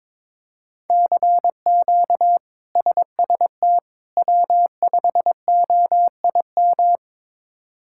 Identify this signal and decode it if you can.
Morse code, heard as a keyed tone, about 22 wpm, 700 Hz. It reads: CQ SST W5OIM
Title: K1USN SST QSO Practice - 22wpm